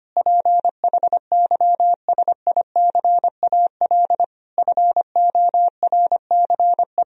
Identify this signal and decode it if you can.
Morse code, heard as a keyed tone, about 25 wpm, 700 Hz. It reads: PHYSICAL FORCE
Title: Sentences from Top 1000 Words - Review Part 4 - 25wpm